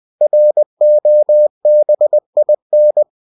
Morse code, heard as a keyed tone, about 20 wpm, 600 Hz. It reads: ROBIN
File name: cw-books / winnie-the-pooh / winnie-20wpm-page-0087